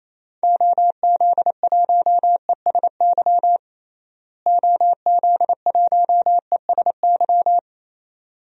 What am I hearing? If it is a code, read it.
Morse code, heard as a keyed tone, about 28 wpm, 700 Hz. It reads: OZ1ESY OZ1ESY
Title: DXCC-100 Call Signs - 6 Characters - Repeated 2x - 28wpm